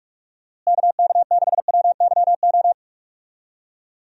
Morse code, heard as a keyed tone, about 46 wpm, 700 Hz. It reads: KK6WYO
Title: USA Call Signs - 6 Characters - Repeated 2x - 45wpm